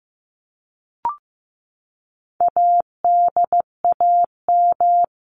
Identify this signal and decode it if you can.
Morse code, heard as a keyed tone, about 15 wpm, 700 Hz. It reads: ADAM
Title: CWT Exchange 15wpm